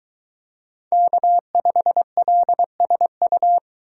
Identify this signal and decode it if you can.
Morse code, heard as a keyed tone, about 22 wpm, 700 Hz. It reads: K5LSU